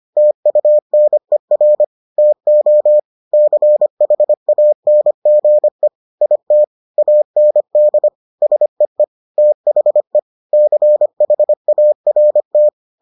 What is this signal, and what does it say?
Morse code, heard as a keyed tone, about 25 wpm, 600 Hz. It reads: TUNER TO CHANGE IT AND SEE THE CHART